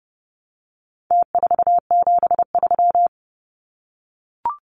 Morse code, heard as a keyed tone, about 30 wpm, 700 Hz. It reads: T473
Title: Numbers - Contest Style - 4-Digit 30wpm